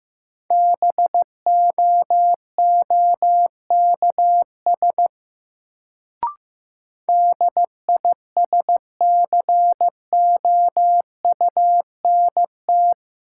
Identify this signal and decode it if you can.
Morse code, heard as a keyed tone, about 15 wpm, 700 Hz. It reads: BOOKS DISCOUNT
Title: Sentences from Top 1200 Words - New Word Drill Alt - 15wpm